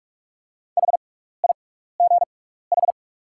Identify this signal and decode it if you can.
Morse code, heard as a keyed tone, about 44 wpm, spaced as for 15 wpm, 690 Hz. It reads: HIGH
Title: 6x Character Spacing - Top 300 Words - 45wpm